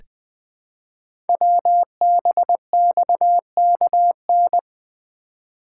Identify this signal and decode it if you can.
Morse code, heard as a keyed tone, about 20 wpm, 700 Hz. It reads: WBXKN
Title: Code Groups - 5 Letters - 20wpm